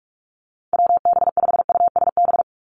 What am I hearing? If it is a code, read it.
Morse code, heard as a keyed tone, about 44 wpm, 700 Hz. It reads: WB5USB
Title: NA SOTA Activator Callsigns 45wpm